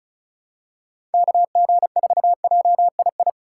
Morse code, heard as a keyed tone, about 36 wpm, 700 Hz. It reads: KG4JII
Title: USA Call Signs - 6 Characters - 2x Character Spacing - 35wpm